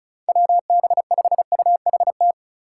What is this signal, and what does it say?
Morse code, heard as a keyed tone, about 34 wpm, 700 Hz. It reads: WB5UHT
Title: USA Call Signs - 3x Character Spacing - 35wpm